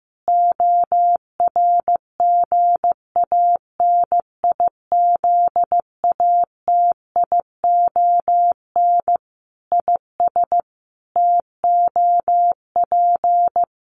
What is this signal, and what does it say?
Morse code, heard as a keyed tone, about 15 wpm, 700 Hz. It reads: ORGANIZATION IS TOP
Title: Sentences from Top 1100 Words 15wpm